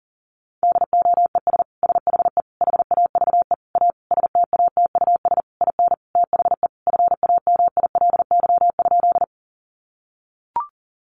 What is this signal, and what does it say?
Morse code, heard as a keyed tone, about 40 wpm, 700 Hz. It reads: DOES SHE HAVE A STATUS IN THE FAMILY?
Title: Sentences from Top 1400 Words 40wpm